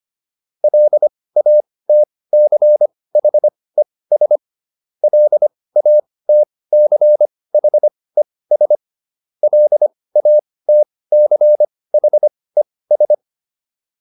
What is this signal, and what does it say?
Morse code, heard as a keyed tone, about 25 wpm, spaced as for 18 wpm, 600 Hz. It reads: LATCHES LATCHES LATCHES